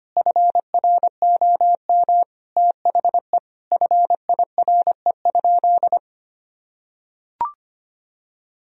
Morse code, heard as a keyed tone, about 25 wpm, 700 Hz. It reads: FROM THE FIRE?
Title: Sentences from Top 1000 Words - Review Part 3 - 25wpm